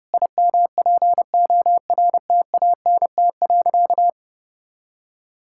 Morse code, heard as a keyed tone, about 32 wpm, 700 Hz. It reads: IMPORTANT.
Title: Sentences from Top 1000 Words - Review Part 5 - 30wpm